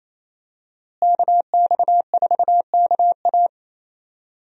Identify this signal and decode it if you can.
Morse code, heard as a keyed tone, about 28 wpm, 700 Hz. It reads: KX4KA
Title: USA Call Signs - 5 Characters - Repeated 2x - 28wpm